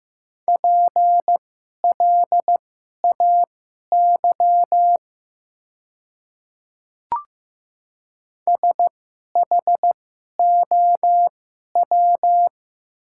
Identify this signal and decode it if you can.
Morse code, heard as a keyed tone, about 15 wpm, spaced as for 11 wpm, 700 Hz. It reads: PLAY SHOW
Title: 2x Character Spacing - Top 200 Words - 15wpm